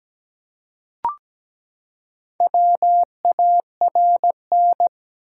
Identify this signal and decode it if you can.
Morse code, heard as a keyed tone, about 17 wpm, 700 Hz. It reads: WARN
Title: Four Letter Words - 17wpm